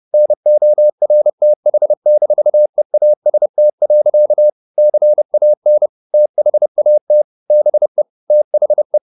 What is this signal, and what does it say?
Morse code, heard as a keyed tone, about 30 wpm, 600 Hz. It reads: NORTH-EAST. CAN THAT BE THE
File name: the old man and the sea / the old man and the sea 1 30wpm